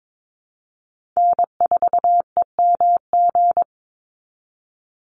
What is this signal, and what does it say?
Morse code, heard as a keyed tone, about 22 wpm, 700 Hz. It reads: N4EMG
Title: NA SOTA Chaser Callsigns 22wpm